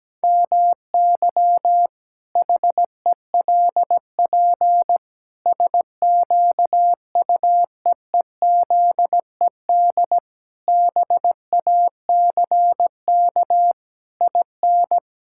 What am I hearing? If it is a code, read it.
Morse code, heard as a keyed tone, about 17 wpm, 700 Hz. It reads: MY HELP SQUEEZED BACK IN
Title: Pangram Challenge v2 - 17wpm